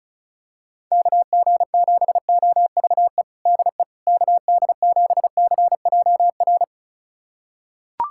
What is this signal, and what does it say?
Morse code, heard as a keyed tone, about 36 wpm, 700 Hz. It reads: KG7OVE DE KD7CJR
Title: USA Callsign DE Callsign 35wpm